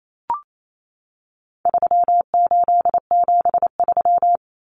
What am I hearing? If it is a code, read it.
Morse code, heard as a keyed tone, about 28 wpm, 700 Hz. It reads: 3873
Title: Numbers - 4-Digit 28wpm